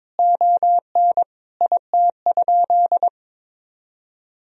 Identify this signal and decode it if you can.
Morse code, heard as a keyed tone, about 22 wpm, 700 Hz. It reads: ON IT?
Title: Sentences from Top 1000 Words - Review Part 1 - 22wpm